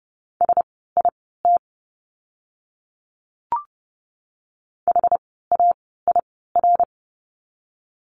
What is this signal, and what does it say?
Morse code, heard as a keyed tone, about 30 wpm, spaced as for 17 wpm, 700 Hz. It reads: SIT HAIR
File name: Lesson 17f - Letter u - Words 30wpm